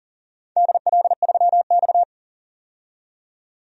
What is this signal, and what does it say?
Morse code, heard as a keyed tone, about 42 wpm, 700 Hz. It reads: DL3X
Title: DXCC-100 Call Signs - 4 Characters - 40wpm